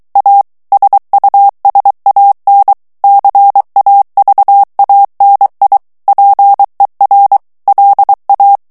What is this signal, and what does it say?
Morse code, heard as a keyed tone, about 23 wpm, 800 Hz. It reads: A SUSAN CAVANI PER LA
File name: morse mp3 / ITALIEN texte clair 1 vitesse 22 mots minute